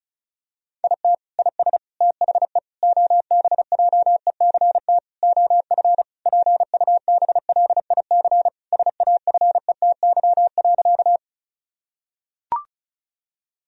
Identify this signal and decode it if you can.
Morse code, heard as a keyed tone, about 34 wpm, 700 Hz. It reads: IT IS THE OBJECT OF PUBLIC SAFETY.
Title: Sentences from Top 1000 Words - Review Part 4 - 35wpm